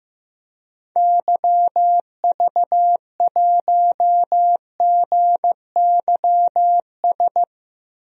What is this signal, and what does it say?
Morse code, heard as a keyed tone, about 16 wpm, 700 Hz. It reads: YV1GYS
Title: DXCC-100 Call Signs - 6 Characters - 15wpm